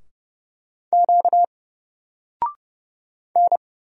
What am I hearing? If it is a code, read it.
Morse code, heard as a keyed tone, about 30 wpm, 700 Hz. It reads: Q N
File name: Single-Letter-Number-30wpm